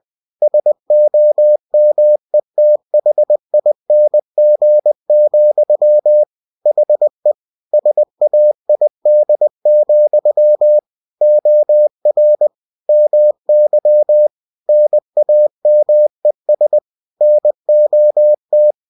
Morse code, heard as a keyed tone, about 20 wpm, 600 Hz. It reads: SOMETHING, HE SAID, OR MY NAMES NOT
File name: cw-books / winnie-the-pooh / winnie-20wpm-page-0028